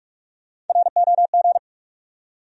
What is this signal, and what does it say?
Morse code, heard as a keyed tone, about 48 wpm, 700 Hz. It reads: ROG
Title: USA Common Names - 45wpm